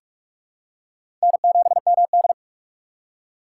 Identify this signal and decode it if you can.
Morse code, heard as a keyed tone, about 45 wpm, 700 Hz. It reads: N7MD
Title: USA Call Signs - 4 Characters - 3x Character Spacing - 45wpm